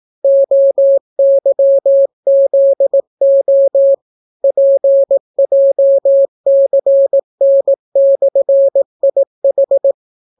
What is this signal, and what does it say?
Morse code, heard as a keyed tone, about 17 wpm, 550 Hz. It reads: OYZO PJCN/IH